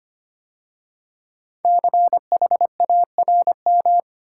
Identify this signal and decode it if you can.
Morse code, heard as a keyed tone, about 25 wpm, 700 Hz. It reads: CHARM
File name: 5 Letter Words - First 3 Letters Given - 25wpm